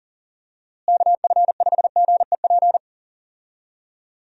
Morse code, heard as a keyed tone, about 37 wpm, 700 Hz. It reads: KF5GEP
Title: USA Call Signs - 6 Characters - Repeated 2x - 40wpm